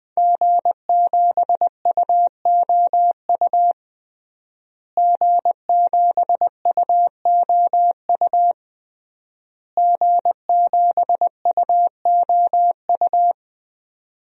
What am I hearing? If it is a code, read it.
Morse code, heard as a keyed tone, about 20 wpm, 700 Hz. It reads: G7UOU G7UOU G7UOU
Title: DXCC-100 Call Signs - 5 Characters - Repeated 3x - 20wpm